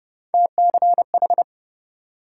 Morse code, heard as a keyed tone, about 41 wpm, spaced as for 36 wpm, 700 Hz. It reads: TCH
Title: Speed-Racing Multi-letter Phonemes - 30wpm